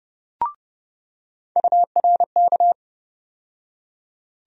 Morse code, heard as a keyed tone, about 30 wpm, 700 Hz. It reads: URK
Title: USA Call Sign Postfixes - 3 Letters 30wpm